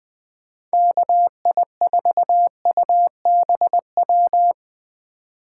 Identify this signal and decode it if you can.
Morse code, heard as a keyed tone, about 20 wpm, 700 Hz. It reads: KI4UBW
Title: Speed Racing - USA 2x3 Call Signs 17wpm